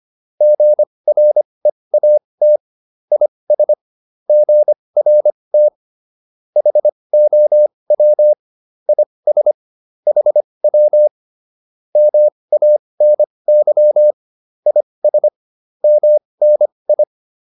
Morse code, heard as a keyed tone, about 25 wpm, spaced as for 20 wpm, 600 Hz. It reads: GREAT IS GRT HOW IS HW MANY IS MNI